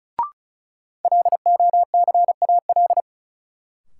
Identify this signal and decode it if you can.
Morse code, heard as a keyed tone, about 36 wpm, 700 Hz. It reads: LOCAL
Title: Sentences from Top 1000 Words - Review Part 3 - New Word Drill - 35wpm